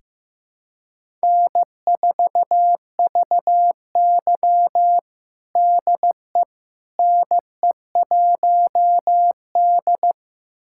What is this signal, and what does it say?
Morse code, heard as a keyed tone, about 15 wpm, 700 Hz. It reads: N4VY DE NE1D
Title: USA Callsign DE Callsign - 4 Characters - 15wpm